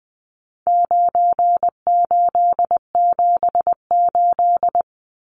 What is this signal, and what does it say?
Morse code, heard as a keyed tone, about 20 wpm, 700 Hz. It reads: 9878
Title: Numbers - 4-Digit 20wpm